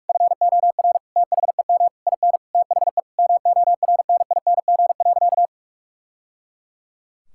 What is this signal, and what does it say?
Morse code, heard as a keyed tone, about 42 wpm, 700 Hz. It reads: FOR THEM IN THE MORNING.
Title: Sentences from Top 1000 Words - Review Part 5 - 45wpm